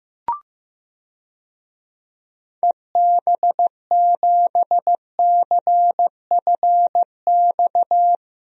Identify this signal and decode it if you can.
Morse code, heard as a keyed tone, about 15 wpm, 700 Hz. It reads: EB7CFX
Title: DXCC-100 Call Signs - 6 Characters - 15wpm